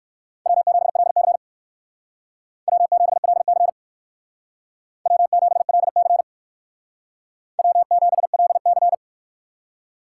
Morse code, heard as a keyed tone, about 58 wpm, 700 Hz. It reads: W7LC W7LC W7LC W7LC